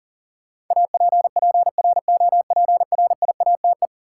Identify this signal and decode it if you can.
Morse code, heard as a keyed tone, about 40 wpm, 700 Hz. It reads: APPROPRIATE
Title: Sentences from Top 900 Words - New Word Drill - 40wpm